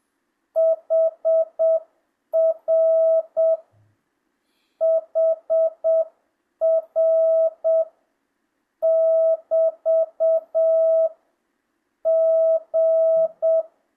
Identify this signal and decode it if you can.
Morse code, heard as a keyed tone, about 7 wpm, 650 Hz. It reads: HR HR <BT> G
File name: jh8cbh / 2025cw